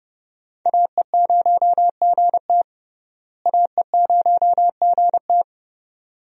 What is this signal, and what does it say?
Morse code, heard as a keyed tone, about 30 wpm, 700 Hz. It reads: AE0GT AE0GT
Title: USA Call Signs - 5 Characters - Repeated 2x - 30wpm